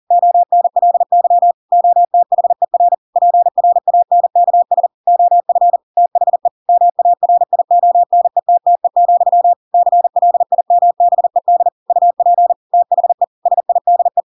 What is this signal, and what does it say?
Morse code, heard as a keyed tone, about 40 wpm, 700 Hz. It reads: ONLY OTHER PRANKS OF THE MARIONETTE, CLIMBED UP THE SIDE
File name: Pinocchio / Pinocchio ch7 40 40 ik1hge